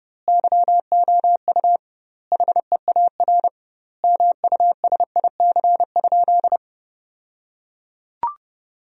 Morse code, heard as a keyed tone, about 30 wpm, 700 Hz. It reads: YOU HEAR MUSIC?
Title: 2x Word-Spacing - Sentences from Top 300 Words - 30wpm